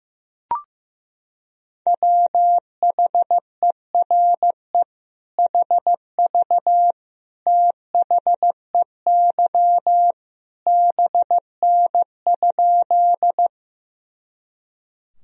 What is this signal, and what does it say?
Morse code, heard as a keyed tone, about 15 wpm, 700 Hz. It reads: WHERE HV THEY BN?
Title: Sentences from Top 200 Words and Abbreviations 15wpm